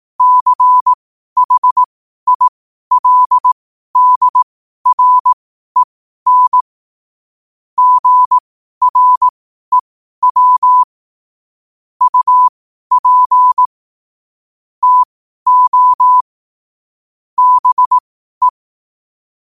Morse code, heard as a keyed tone, about 18 wpm, spaced as for 12 wpm, 1000 Hz. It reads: CHILDREN GREW UP TO BE
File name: qotdfiles / 12wpm